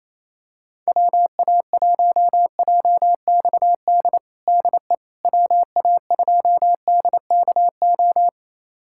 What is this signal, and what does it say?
Morse code, heard as a keyed tone, about 27 wpm, 700 Hz. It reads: WA1JXD DE WA2DKO
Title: USA Callsign DE Callsign - 6 Characters - 28wpm